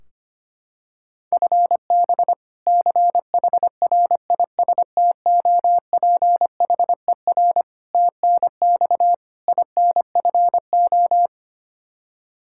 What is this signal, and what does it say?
Morse code, heard as a keyed tone, about 25 wpm, 700 Hz. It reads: FB CHRISTOPHER TNX INFO
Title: K1USN SST QSO Practice 25wpm